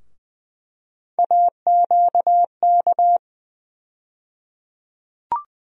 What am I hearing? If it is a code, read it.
Morse code, heard as a keyed tone, about 20 wpm, 700 Hz. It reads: AQK
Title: USA Call Sign Postfixes - 3 Letters 20wpm